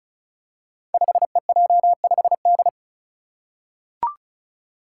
Morse code, heard as a keyed tone, about 36 wpm, 700 Hz. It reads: 5EJ5D
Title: Five Challenging Letter-Number Combinations 35wpm